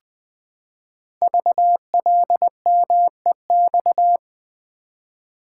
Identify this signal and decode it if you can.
Morse code, heard as a keyed tone, about 20 wpm, 700 Hz. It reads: VLMEX
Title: Code Groups - 5 Letters - 20wpm